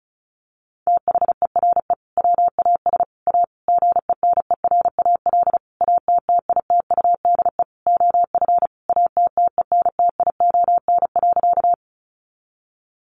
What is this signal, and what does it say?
Morse code, heard as a keyed tone, about 34 wpm, 700 Hz. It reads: THERE WAS A GENERAL ATTITUDE OF ATTENTION.
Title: Sentences from Top 1300 Words 35wpm